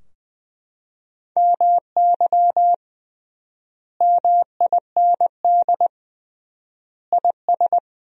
Morse code, heard as a keyed tone, about 21 wpm, 700 Hz. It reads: MY MIND IS
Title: 3x Word-Spacing - Sentences from Top 500 Words - 20wpm